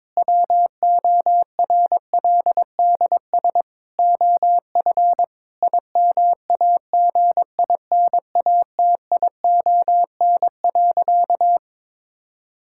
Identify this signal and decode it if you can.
Morse code, heard as a keyed tone, about 21 wpm, 700 Hz. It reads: WORLDS OF IMAGINATION.
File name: Difficult Sentences with Hints - 22wpm